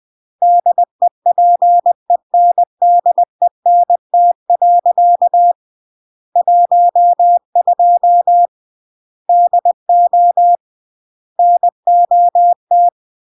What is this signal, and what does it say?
Morse code, heard as a keyed tone, about 20 wpm, 700 Hz. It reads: DEPENDENT. 12 DO NOT